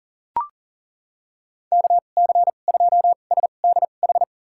Morse code, heard as a keyed tone, about 38 wpm, spaced as for 28 wpm, 700 Hz. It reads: KC2SDH
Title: USA Call Signs - 6 Characters - 2x Character Spacing - 40wpm